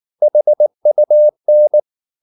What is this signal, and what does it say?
Morse code, heard as a keyed tone, about 19 wpm, 600 Hz. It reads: HUN
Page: Ferie 19 wpm